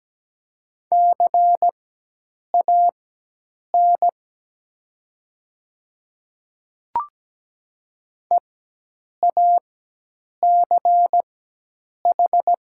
Morse code, heard as a keyed tone, about 17 wpm, spaced as for 8 wpm, 700 Hz. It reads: CAN EACH
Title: 4x Character Spacing - Top 500 Words - 17wpm